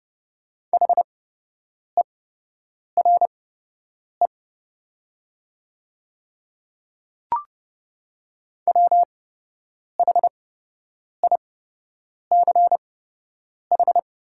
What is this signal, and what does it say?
Morse code, heard as a keyed tone, about 30 wpm, spaced as for 8 wpm, 700 Hz. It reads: HERE WHICH
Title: 8x Character Spacing - Top 300 Words - 30wpm